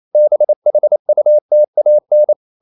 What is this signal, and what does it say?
Morse code, heard as a keyed tone, about 28 wpm, 600 Hz. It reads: BHUTAN
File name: CW / Practice / NationalCapitals28